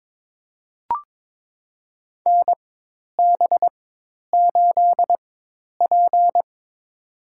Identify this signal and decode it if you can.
Morse code, heard as a keyed tone, about 21 wpm, spaced as for 10 wpm, 700 Hz. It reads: NB8P
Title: USA Call Signs - 4 Characters - 4x Character Spacing - 22wpm